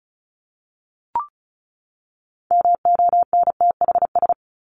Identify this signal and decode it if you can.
Morse code, heard as a keyed tone, about 34 wpm, 700 Hz. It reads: MONTHS
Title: ES:S Suffix - Top 100 Words with Base Words 35wpm